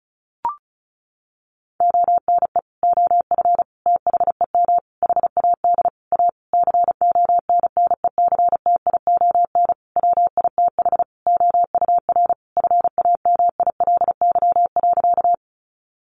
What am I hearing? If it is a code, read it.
Morse code, heard as a keyed tone, about 34 wpm, 700 Hz. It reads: ONE OF THEM HAD A CONNECTION WITH OUR FAMILY.
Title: Sentences from Top 1300 Words 35wpm